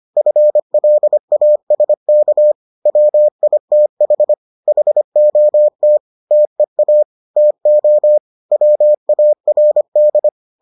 Morse code, heard as a keyed tone, about 25 wpm, 600 Hz. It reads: FLASK WITH HOT TEA TO WARD